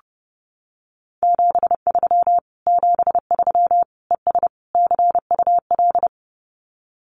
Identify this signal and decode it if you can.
Morse code, heard as a keyed tone, about 28 wpm, 700 Hz. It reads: 73 73 ES CUL
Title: Common QSO Phrases Encore 30wpm